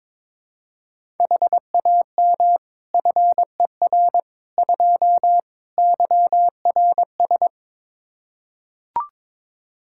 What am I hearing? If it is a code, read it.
Morse code, heard as a keyed tone, about 22 wpm, 700 Hz. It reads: HAM FER 2 YRS
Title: Practice QSO - Exchange 3 - More Info - 22wpm